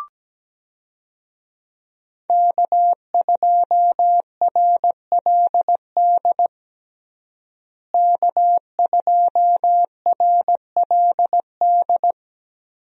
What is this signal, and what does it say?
Morse code, heard as a keyed tone, about 17 wpm, 700 Hz. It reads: K2RLD K2RLD
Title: USA Call Signs - 5 Characters - Repeated 2x - 17wpm